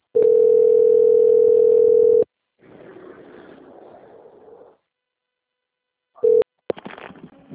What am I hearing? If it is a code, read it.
Ringback tone (North America), 440 + 480 Hz.